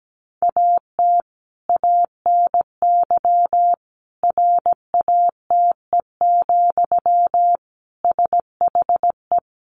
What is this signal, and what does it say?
Morse code, heard as a keyed tone, about 17 wpm, 700 Hz. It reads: AT ANY RATE, SHE
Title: Sentences from Top 1300 Words 17wpm